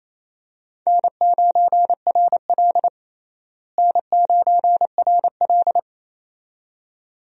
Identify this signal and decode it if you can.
Morse code, heard as a keyed tone, about 28 wpm, 700 Hz. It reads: N9RL N9RL
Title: USA Call Signs - 4 Characters - Repeated 2x - 28wpm